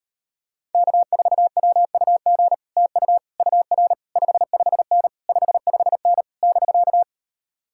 Morse code, heard as a keyed tone, about 33 wpm, 700 Hz. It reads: K4WUG TU UR 55N 55N <BK>